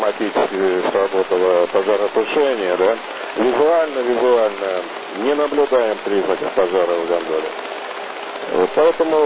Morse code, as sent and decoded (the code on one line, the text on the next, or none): none